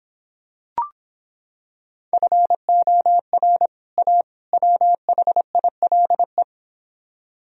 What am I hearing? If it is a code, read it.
Morse code, heard as a keyed tone, about 26 wpm, 700 Hz. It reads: FOR A WHILE